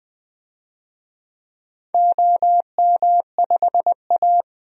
Morse code, heard as a keyed tone, about 20 wpm, 700 Hz. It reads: OM5A
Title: DXCC-100 Call Signs - 4 Characters - 20wpm